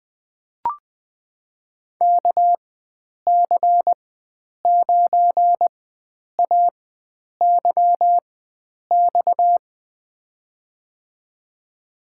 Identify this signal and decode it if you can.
Morse code, heard as a keyed tone, about 20 wpm, spaced as for 9 wpm, 700 Hz. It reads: KC9AYX
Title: USA Call Signs - 6 Characters - 4x Character Spacing - 20wpm